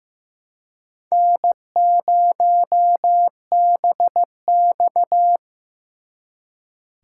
Morse code, heard as a keyed tone, about 18 wpm, 700 Hz. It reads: N0BX
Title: Speed Racing - USA 1x2 Call Signs 15wpm